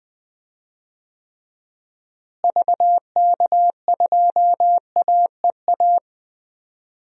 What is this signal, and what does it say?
Morse code, heard as a keyed tone, about 21 wpm, 700 Hz. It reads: VK2AEA
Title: DXCC-100 Call Signs - 6 Characters - Repeated 2x - 20wpm